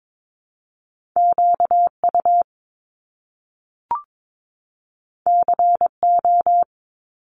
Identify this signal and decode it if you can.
Morse code, heard as a keyed tone, about 22 wpm, 700 Hz. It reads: QU CO
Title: Top Two Letter Combinations in Words 22wpm